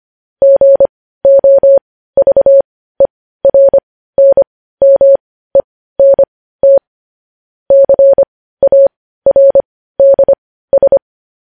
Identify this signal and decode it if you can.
Morse code, heard as a keyed tone, about 25 wpm, spaced as for 15 wpm, 550 Hz. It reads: GOVERNMENT CARDS
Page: news headlines in morse code at 15 wpm